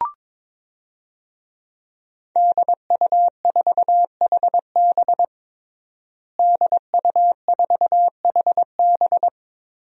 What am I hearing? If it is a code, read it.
Morse code, heard as a keyed tone, about 22 wpm, 700 Hz. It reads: DU4HB DU4HB
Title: DXCC-100 Call Signs - 5 Characters - Repeated 3x - 22wpm